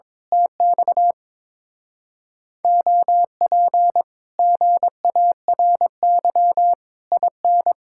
Morse code, heard as a keyed tone, about 22 wpm, 700 Hz. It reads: TX OP GARY IN
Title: Speed-Racing Common QSO Phrases Encore - 22wpm